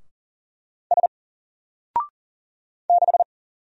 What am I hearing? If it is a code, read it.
Morse code, heard as a keyed tone, about 40 wpm, 700 Hz. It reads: S 6
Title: Single Character (Letter:Number:Punctuation:Prosign) Alt 40wpm